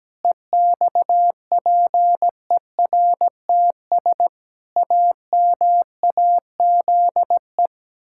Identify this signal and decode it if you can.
Morse code, heard as a keyed tone, about 17 wpm, 700 Hz. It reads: EXPERTS AMAZE
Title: Pangram Challenge v2 - 17wpm